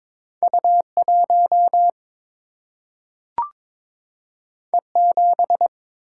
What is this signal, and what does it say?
Morse code, heard as a keyed tone, about 22 wpm, 700 Hz. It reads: U1 E7
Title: Single-Letter Single-Number 22wpm